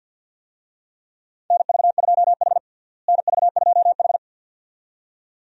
Morse code, heard as a keyed tone, about 52 wpm, 700 Hz. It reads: NV2H NV2H
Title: USA Call Signs - 4 Characters - Repeated 2x - 50wpm